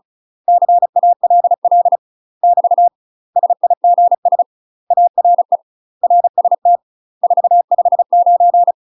Morse code, heard as a keyed tone, about 37 wpm, 700 Hz. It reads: CALL <BT> SIGS ARE RST 459
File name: UK MASTER QSOs TRAINING / 35WPM / UK MASTER QSOs 0003